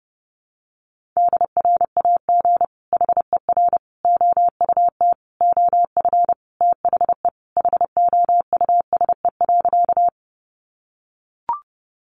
Morse code, heard as a keyed tone, about 28 wpm, 700 Hz. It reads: DRAG HER OUT OF THE HOUSE.
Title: Sentences from Top 1500 Words 30wpm